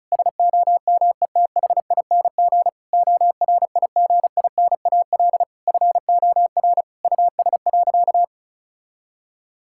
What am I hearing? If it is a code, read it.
Morse code, heard as a keyed tone, about 36 wpm, 700 Hz. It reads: SOMETHING ORIGINAL FOR US.
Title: Sentences from Top 1000 Words - Review Part 6 - 35wpm